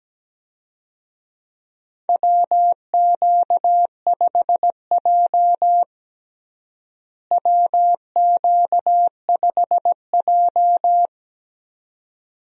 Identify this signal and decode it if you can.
Morse code, heard as a keyed tone, about 17 wpm, 700 Hz. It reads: WQ5J WQ5J
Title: USA Call Signs - 4 Characters - Repeated 2x - 17wpm